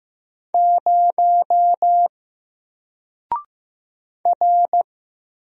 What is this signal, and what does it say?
Morse code, heard as a keyed tone, about 15 wpm, 700 Hz. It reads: 0 R
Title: Single Letter-Number Alt 15wpm